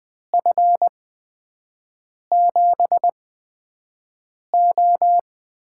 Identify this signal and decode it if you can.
Morse code, heard as a keyed tone, about 20 wpm, spaced as for 5 wpm, 700 Hz. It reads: F7O
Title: USA Call Signs - 6 Characters - 8x Character Spacing - 20wpm